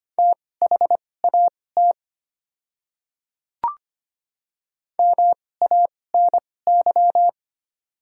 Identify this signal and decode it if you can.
Morse code, heard as a keyed tone, about 25 wpm, spaced as for 18 wpm, 700 Hz. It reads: THAT MANY
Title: 2x Character Spacing - Top 300 Words - 25wpm